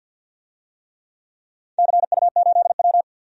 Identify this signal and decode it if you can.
Morse code, heard as a keyed tone, about 51 wpm, 700 Hz. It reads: CU8W
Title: DXCC-100 Call Signs - 4 Characters - 50wpm